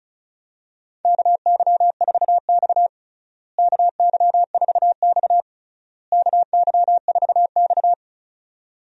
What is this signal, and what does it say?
Morse code, heard as a keyed tone, about 34 wpm, 700 Hz. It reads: KY4X KY4X KY4X
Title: USA Call Signs - 4 Characters - Repeated 3x - 35wpm